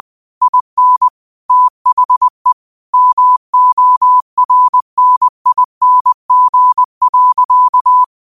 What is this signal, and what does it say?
Morse code, heard as a keyed tone, about 20 wpm, 1000 Hz. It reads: IN THE MORNING.